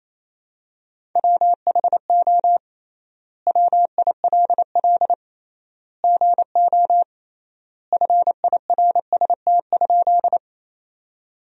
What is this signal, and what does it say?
Morse code, heard as a keyed tone, about 28 wpm, 700 Hz. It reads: WHO WILL GO FIRST?
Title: 3x Word-Spacing - Sentences from Top 100 Words - 28wpm